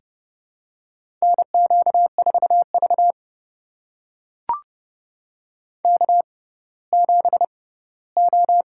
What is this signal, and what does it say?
Morse code, heard as a keyed tone, about 29 wpm, spaced as for 17 wpm, 700 Hz. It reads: NQ4V K7O
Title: USA Call Signs - 4 Characters - 6x Character Spacing - 30wpm